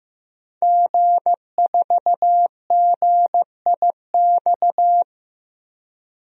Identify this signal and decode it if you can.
Morse code, heard as a keyed tone, about 15 wpm, 700 Hz. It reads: G4GIX
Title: DXCC-100 Call Signs - 5 Characters - 15wpm